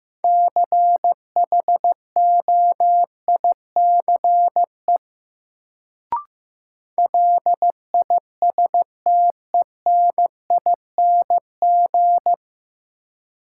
Morse code, heard as a keyed tone, about 16 wpm, 700 Hz. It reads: CHOICE LISTENING
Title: Sets of 5 Words - New Word Drill - 15wpm